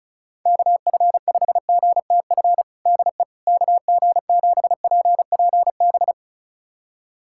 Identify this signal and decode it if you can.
Morse code, heard as a keyed tone, about 36 wpm, 700 Hz. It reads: KF5GTF DE KG7PPB
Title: USA Callsign DE Callsign 35wpm